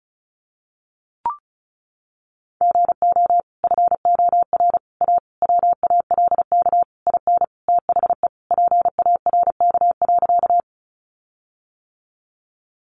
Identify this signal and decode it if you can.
Morse code, heard as a keyed tone, about 34 wpm, 700 Hz. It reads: GO FOR A WALK IN THE PARK.
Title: Sentences from Top 1300 Words 35wpm